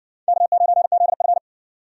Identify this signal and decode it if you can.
Morse code, heard as a keyed tone, about 60 wpm, 700 Hz. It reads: K9ZF